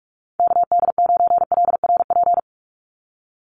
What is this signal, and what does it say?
Morse code, heard as a keyed tone, about 44 wpm, 700 Hz. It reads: KD9LRP
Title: USA Call Signs 45wpm